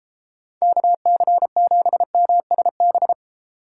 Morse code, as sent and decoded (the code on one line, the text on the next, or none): -.- -.-. --... -- ... -...
KC7MSB